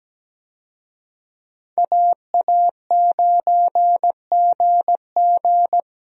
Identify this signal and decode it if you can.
Morse code, heard as a keyed tone, about 17 wpm, 700 Hz. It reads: AA9GG
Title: USA Call Signs - 5 Characters - Repeated 2x - 17wpm